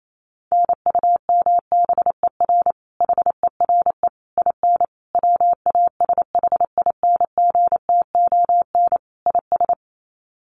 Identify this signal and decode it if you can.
Morse code, heard as a keyed tone, about 28 wpm, 700 Hz. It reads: NUMBER HERE IN WASHINGTON IS
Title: Sets of 5 Words Encore2 28wpm